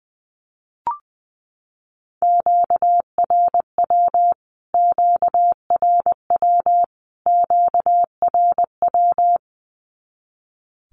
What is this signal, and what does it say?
Morse code, heard as a keyed tone, about 20 wpm, 700 Hz. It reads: QRW QRW QRW
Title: Q-Codes Repeated 3x 20wpm